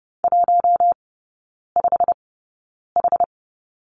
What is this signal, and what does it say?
Morse code, heard as a keyed tone, about 30 wpm, spaced as for 9 wpm, 700 Hz. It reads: 15H
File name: course / 30:10wpm / 087 Lesson 15e - Number 5 - Three Characters 30wpm